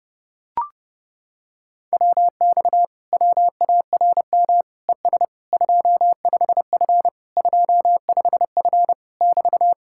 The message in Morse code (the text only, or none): WX WARM ES 25F 25F <BT>